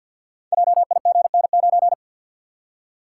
Morse code, heard as a keyed tone, about 49 wpm, 700 Hz. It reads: JIGN9
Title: Code Groups - 5 Letter:Numbers - 50wpm